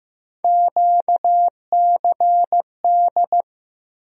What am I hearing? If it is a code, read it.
Morse code, heard as a keyed tone, about 15 wpm, 700 Hz. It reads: QCD
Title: USA Call Sign Postfixes - 3 Letters 15wpm